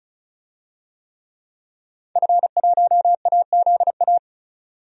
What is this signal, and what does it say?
Morse code, heard as a keyed tone, about 36 wpm, 700 Hz. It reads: F1AZA
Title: DXCC-100 Call Signs - 5 Characters - 35wpm